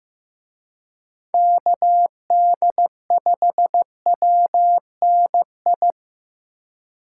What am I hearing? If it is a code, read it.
Morse code, heard as a keyed tone, about 18 wpm, 700 Hz. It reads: KD5WNI
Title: Speed Racing - USA 2x3 Call Signs 15wpm